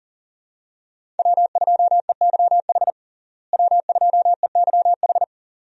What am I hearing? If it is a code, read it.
Morse code, heard as a keyed tone, about 40 wpm, 700 Hz. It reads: W2EYH W2EYH
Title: USA Call Signs - 5 Characters - Repeated 2x - 40wpm